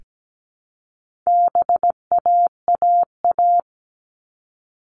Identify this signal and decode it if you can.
Morse code, heard as a keyed tone, about 17 wpm, 700 Hz. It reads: BAAA
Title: All Cut Numbers - 4-Digit 17wpm